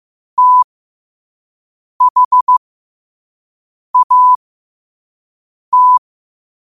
Morse code, heard as a keyed tone, about 15 wpm, spaced as for 5 wpm, 1000 Hz. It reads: THAT
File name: 5wpm